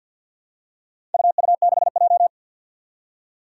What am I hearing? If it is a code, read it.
Morse code, heard as a keyed tone, about 50 wpm, 700 Hz. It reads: UU6J
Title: Four Challenging Letter-Number Combinations 50wpm